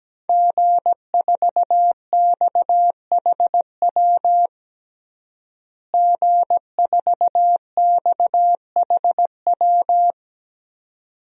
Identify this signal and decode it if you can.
Morse code, heard as a keyed tone, about 17 wpm, 700 Hz. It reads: G4XHW G4XHW
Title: DXCC-100 Call Signs - 5 Characters - Repeated 2x - 17wpm